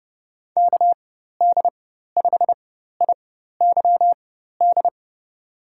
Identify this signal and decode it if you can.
Morse code, heard as a keyed tone, about 30 wpm, spaced as for 14 wpm, 700 Hz. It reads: KD5IYD
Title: USA Call Signs - 4x Character Spacing - 30wpm